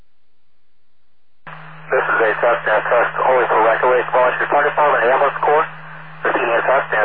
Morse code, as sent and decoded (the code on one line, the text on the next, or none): none